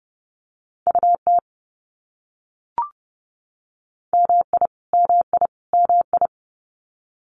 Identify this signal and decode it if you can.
Morse code, heard as a keyed tone, about 30 wpm, 700 Hz. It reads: UT MI MI MI
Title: US States Repeated 3x 30wpm